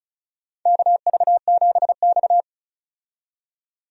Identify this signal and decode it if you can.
Morse code, heard as a keyed tone, about 36 wpm, 700 Hz. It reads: KV7X
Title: USA Call Signs - 4 Characters - 35wpm